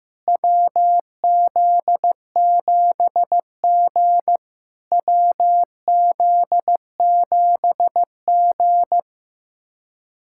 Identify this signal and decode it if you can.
Morse code, heard as a keyed tone, about 15 wpm, 700 Hz. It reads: WZ7G WZ7G
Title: SOTA QSO Practice 15wpm